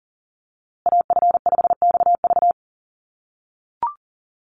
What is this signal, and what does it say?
Morse code, heard as a keyed tone, about 40 wpm, 700 Hz. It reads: AF5XV
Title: USA Call Signs Repeated 2x 40wpm